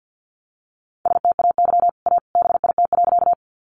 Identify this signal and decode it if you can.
Morse code, heard as a keyed tone, about 50 wpm, 700 Hz. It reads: STAY A BIT.
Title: Sentences from Top 900 Words 50wpm